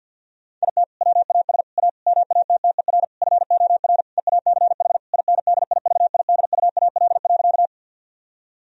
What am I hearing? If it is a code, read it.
Morse code, heard as a keyed tone, about 50 wpm, 700 Hz. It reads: IT WAS A MATTER FOR EACH INDIVIDUAL.
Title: Sentences from Top 1000 Words - Review Part 4 - 50wpm